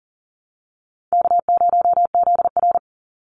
Morse code, heard as a keyed tone, about 40 wpm, 700 Hz. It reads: K0ZR
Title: Easy Call Signs Repeated 3x - 40wpm